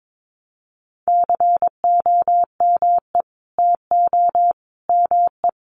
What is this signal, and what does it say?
Morse code, heard as a keyed tone, about 22 wpm, 700 Hz. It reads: COME TO ME
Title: Sentences from Top 900 Words 22wpm